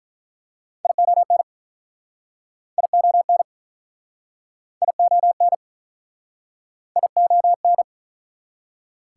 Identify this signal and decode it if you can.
Morse code, heard as a keyed tone, about 48 wpm, 700 Hz. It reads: ION ION ION ION